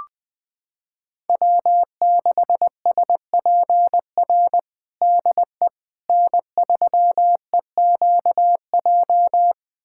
Morse code, heard as a keyed tone, about 19 wpm, 700 Hz. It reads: W6SPR DE N3EQJ
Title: USA Callsign DE Callsign - 5 Characters - 20wpm